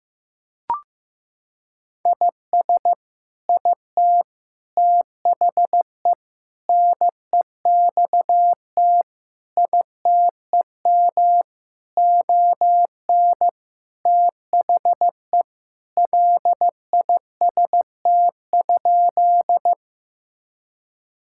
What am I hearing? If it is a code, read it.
Morse code, heard as a keyed tone, about 15 wpm, 700 Hz. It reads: IS IT THE NEXT ITEM ON THE LIST?
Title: Sentences from Top 1000 Words - Review Part 6 - 15wpm